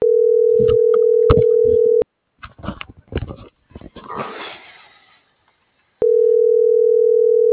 Ringback tone (North America), 440 + 480 Hz.